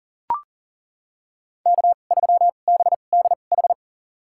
Morse code, heard as a keyed tone, about 40 wpm, spaced as for 29 wpm, 700 Hz. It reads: K3BDH
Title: USA Call Signs - 2x Character Spacing - 40wpm